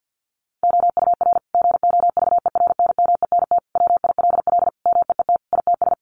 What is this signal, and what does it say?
Morse code, heard as a keyed tone, about 50 wpm, 700 Hz. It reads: OUR GOVERNMENT WILL MEET ITS
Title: Sentences from Top 1200 Words 50wpm